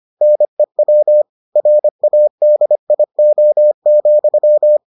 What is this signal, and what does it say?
Morse code, heard as a keyed tone, about 25 wpm, 600 Hz. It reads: NEW RADIO,